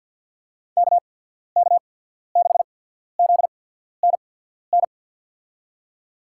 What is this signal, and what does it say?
Morse code, heard as a keyed tone, about 54 wpm, spaced as for 14 wpm, 700 Hz. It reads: KK6ZNN
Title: USA Call Signs - 6 Characters - 8x Character Spacing - 50wpm